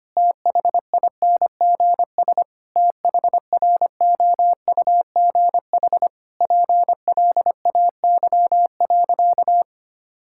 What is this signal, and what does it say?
Morse code, heard as a keyed tone, about 25 wpm, 700 Hz. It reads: THINGS THROUGH PLAY.
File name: practice / 25wpm / Difficult Sentences with Hints - 25wpm